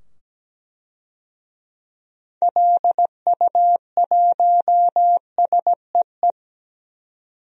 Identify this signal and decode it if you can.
Morse code, heard as a keyed tone, about 17 wpm, 700 Hz. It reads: LU1SEE
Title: DXCC-100 Call Signs - 6 Characters - 17wpm